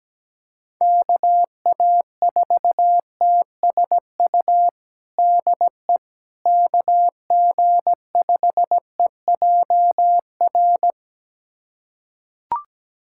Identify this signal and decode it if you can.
Morse code, heard as a keyed tone, about 17 wpm, 700 Hz. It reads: KA4TSU DE KG5EJR
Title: USA Callsign DE Callsign - 6 Characters - 17wpm